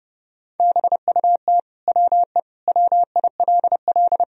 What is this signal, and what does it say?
Morse code, heard as a keyed tone, about 30 wpm, 700 Hz. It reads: BUT WE WILL
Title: Sentences from Top 1000 Words - Review Part 2 - 30wpm